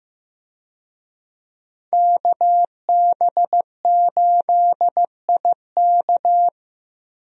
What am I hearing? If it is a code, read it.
Morse code, heard as a keyed tone, about 15 wpm, 700 Hz. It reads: KB8IK
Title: USA Call Signs - 5 Characters - Repeated 3x - 15wpm